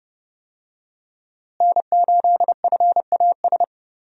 Morse code, heard as a keyed tone, about 30 wpm, 700 Hz. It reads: N8FAS
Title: USA Call Signs - 5 Characters - Repeated 2x - 30wpm